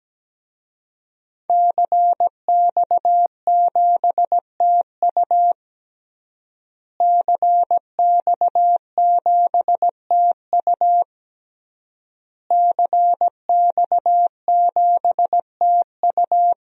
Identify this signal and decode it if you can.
Morse code, heard as a keyed tone, about 17 wpm, 700 Hz. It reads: CX7TU CX7TU CX7TU
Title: DXCC-100 Call Signs - 5 Characters - Repeated 3x - 17wpm